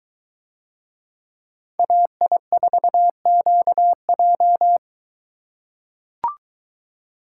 Morse code, heard as a keyed tone, about 22 wpm, 700 Hz. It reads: AI4QJ